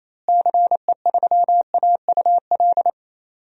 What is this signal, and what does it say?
Morse code, heard as a keyed tone, about 28 wpm, 700 Hz. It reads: CE3AUL
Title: DXCC-100 Call Signs - 6 Characters - Repeated 2x - 28wpm